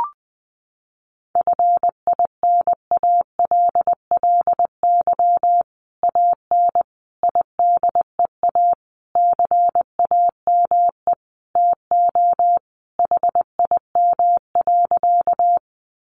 Morse code, heard as a keyed tone, about 20 wpm, 700 Hz. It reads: FINALLY AN IDEA CAME TO HIM.
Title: Sentences from Top 1000 Words 20wpm